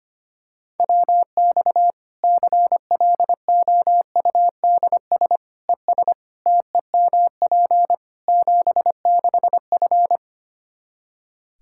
Morse code, heard as a keyed tone, about 25 wpm, 700 Hz. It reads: WX CLOUDS ES TEMP 76F
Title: K1USN SST QSO Practice 25wpm